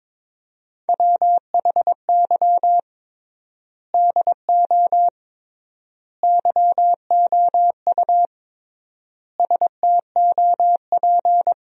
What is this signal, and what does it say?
Morse code, heard as a keyed tone, about 22 wpm, 700 Hz. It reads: WHY DO YOU STOP
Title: 3x Word-Spacing - Sentences from Top 300 Words - 22wpm